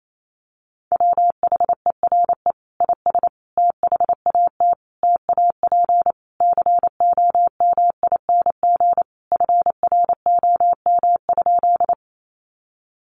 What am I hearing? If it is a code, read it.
Morse code, heard as a keyed tone, about 28 wpm, 700 Hz. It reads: WHERE IS THAT TAP COMING FROM?
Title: Sentences from Top 1400 Words 28wpm